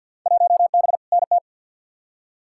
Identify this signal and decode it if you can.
Morse code, heard as a keyed tone, about 53 wpm, 700 Hz. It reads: 1B NT
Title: Field Day Exchanges Encore - 50wpm